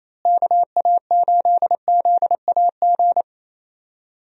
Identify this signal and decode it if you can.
Morse code, heard as a keyed tone, about 28 wpm, 700 Hz. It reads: KA8ZAG
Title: USA Call Signs - 6 Characters - Repeated 2x - 28wpm